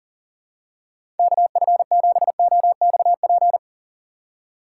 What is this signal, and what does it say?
Morse code, heard as a keyed tone, about 37 wpm, 700 Hz. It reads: KF7OXP
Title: USA Call Signs - 6 Characters - 6x Character Spacing - 40wpm